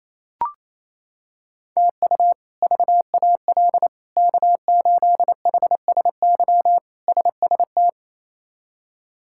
Morse code, heard as a keyed tone, about 28 wpm, 700 Hz. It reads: TU VAL K8HSY SST
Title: K1USN SST QSO Practice - 28wpm